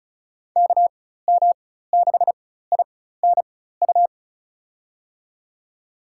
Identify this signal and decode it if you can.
Morse code, heard as a keyed tone, about 34 wpm, spaced as for 16 wpm, 700 Hz. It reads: KM6INU
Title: USA Call Signs - 6 Characters - 4x Character Spacing - 35wpm